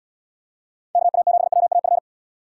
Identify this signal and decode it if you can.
Morse code, heard as a keyed tone, about 71 wpm, 700 Hz. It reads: KA7WAF